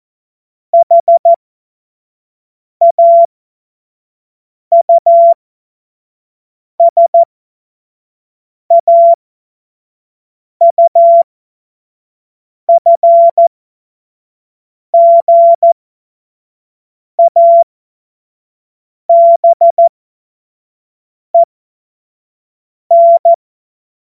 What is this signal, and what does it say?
Morse code, heard as a keyed tone, about 14 wpm, spaced as for 5 wpm, 680 Hz. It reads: HAUSAUFGABEN
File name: Morsekurs / Witze / Witz 28. September 2021, 25 F70